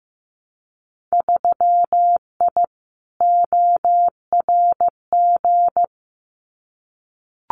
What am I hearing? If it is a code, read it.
Morse code, heard as a keyed tone, about 15 wpm, 700 Hz. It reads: 3I ORG
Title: Winter Field Day Exchanges Encore 15wpm